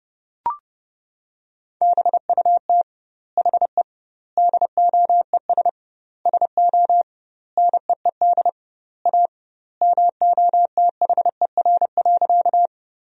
Morse code, heard as a keyed tone, about 30 wpm, 700 Hz. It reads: BUT HE DOES SO NEED A MOTHER.
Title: 2x Word-Spacing - Sentences from Top 200 Words - 30wpm